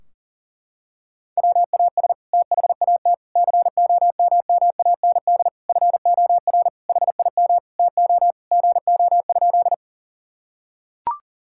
Morse code, heard as a keyed tone, about 37 wpm, 700 Hz. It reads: WAS THAT COMMAND FOR HIM TO GO?
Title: Sentences from Top 1600 Words 40wpm